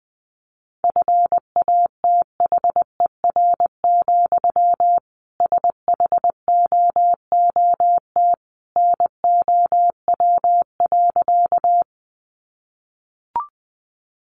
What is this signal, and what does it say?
Morse code, heard as a keyed tone, about 20 wpm, 700 Hz. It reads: FATHER, SHOOT NOW.
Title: Sentences from Top 1200 Words 20wpm